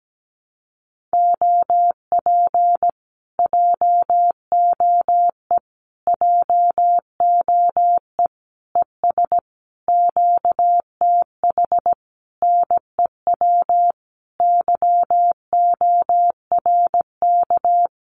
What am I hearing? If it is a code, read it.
Morse code, heard as a keyed tone, about 17 wpm, 700 Hz. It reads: OP JOE JOE ES QTH NEW YORK
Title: Common QSO Phrases Encore 17wpm